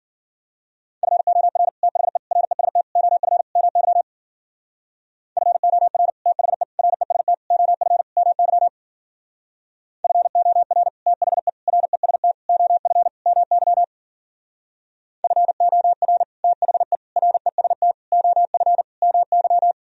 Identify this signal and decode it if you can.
Morse code, heard as a keyed tone, about 49 wpm, 700 Hz. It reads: FOR THE REST OF MY FOR THE REST OF MY FOR THE REST OF MY FOR THE REST OF MY
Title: Speed-Racing - Sets of 5 Words 40wpm